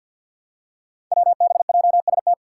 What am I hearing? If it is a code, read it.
Morse code, heard as a keyed tone, about 48 wpm, 700 Hz. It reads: WBJST
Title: Five Challenging Letter-Number Combinations 50wpm